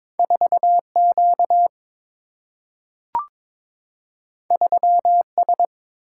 Morse code, heard as a keyed tone, about 22 wpm, 700 Hz. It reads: 4 Q 3 S
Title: Single-Letter Single-Number 22wpm